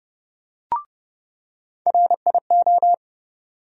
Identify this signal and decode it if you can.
Morse code, heard as a keyed tone, about 30 wpm, 700 Hz. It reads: RIO
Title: USA Call Sign Postfixes - 3 Letters 30wpm